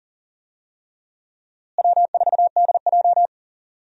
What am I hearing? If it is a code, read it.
Morse code, heard as a keyed tone, about 40 wpm, 700 Hz. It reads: W4DJ
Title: USA Call Signs - 4 Characters - Repeated 3x - 40wpm